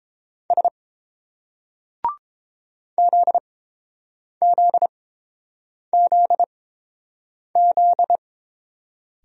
Morse code, heard as a keyed tone, about 27 wpm, spaced as for 7 wpm, 700 Hz. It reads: S ZZZZ
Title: Speed-Racing Single Letters 22wpm